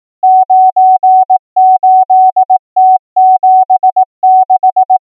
Morse code, heard as a keyed tone, about 18 wpm, 750 Hz. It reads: 98T76
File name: numbers / V2-morse